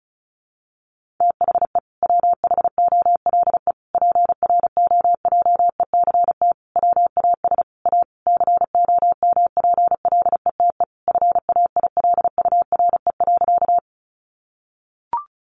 Morse code, heard as a keyed tone, about 34 wpm, 700 Hz. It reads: THE WHOLE PROJECT WAS A COMPLETE FAILURE.
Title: Sentences from Top 1400 Words 35wpm